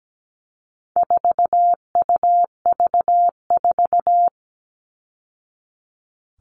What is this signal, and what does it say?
Morse code, heard as a keyed tone, about 17 wpm, 700 Hz. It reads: 4UV4
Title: All Cut Numbers - 4-Digit 17wpm